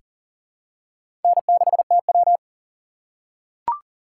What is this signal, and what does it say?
Morse code, heard as a keyed tone, about 40 wpm, 700 Hz. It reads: N6TW
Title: K1USN SST QSO Practice - 40wpm